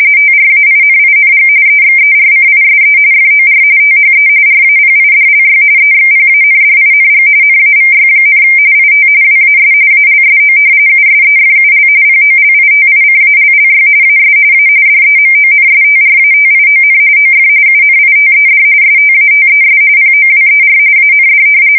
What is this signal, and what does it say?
RTTY, about 45.45 baud. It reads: TRAINING EXERCISE FOR THE RAPIDLY EXPANDING NAVY, IT WAS ALSO USEFUL POLITICALLY AS A DEMONSTRATION OF UNITED STATES POWER AND
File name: fsk170 / usnavy3